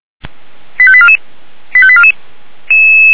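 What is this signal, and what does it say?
Five-tone selective call (ZVEI), heard as quick runs of five tones, tones of about 70 ms. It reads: 85844 85844
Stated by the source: ZVEI five-tone selective call received on an SDR.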